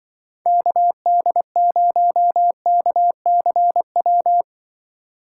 Morse code, heard as a keyed tone, about 23 wpm, 700 Hz. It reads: KD0KCW
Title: Speed Racing - USA 2x3 Call Signs 20wpm